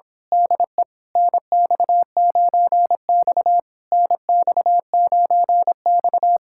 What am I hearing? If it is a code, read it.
Morse code, heard as a keyed tone, about 26 wpm, 700 Hz. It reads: DE NX9X NX9X
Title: Speed-Racing Common QSO Phrases - 22wpm